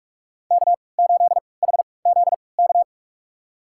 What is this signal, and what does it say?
Morse code, heard as a keyed tone, about 43 wpm, spaced as for 25 wpm, 700 Hz. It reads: K8HZK
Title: USA Call Signs - 3x Character Spacing - 45wpm